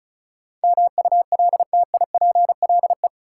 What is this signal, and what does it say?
Morse code, heard as a keyed tone, about 36 wpm, 700 Hz. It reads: MULTIPLE
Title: Sets of 5 Words - New Word Drill - 35wpm